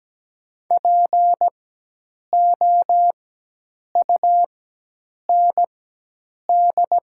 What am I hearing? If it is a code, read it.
Morse code, heard as a keyed tone, about 17 wpm, spaced as for 8 wpm, 700 Hz. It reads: POUND
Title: 4x Character Spacing - Top 500 Words - 17wpm